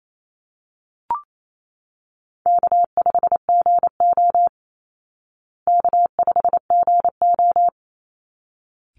Morse code, heard as a keyed tone, about 28 wpm, 700 Hz. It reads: K5GO K5GO
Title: Call Signs Repeated 3x - 28wpm